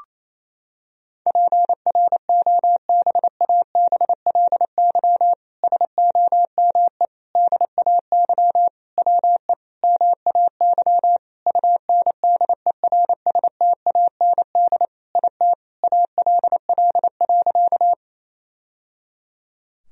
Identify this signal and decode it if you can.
Morse code, heard as a keyed tone, about 28 wpm, 700 Hz. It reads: PROBABLY SOME DAY WE MAY UNDERSTAND IT ALL.
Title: Sentences from Top 1000 Words - Review Part 4 - 28wpm